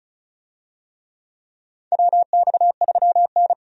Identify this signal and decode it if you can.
Morse code, heard as a keyed tone, about 36 wpm, 700 Hz. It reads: WX3N
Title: USA Call Signs - 4 Characters - 35wpm